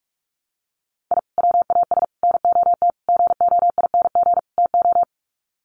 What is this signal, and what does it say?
Morse code, heard as a keyed tone, about 44 wpm, 700 Hz. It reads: I WAS NOT GOING TO
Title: Sets of 5 Words 45wpm